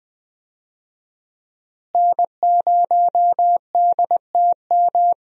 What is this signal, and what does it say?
Morse code, heard as a keyed tone, about 20 wpm, 700 Hz. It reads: N0DTM
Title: USA Call Signs - 5 Characters - Repeated 3x - 20wpm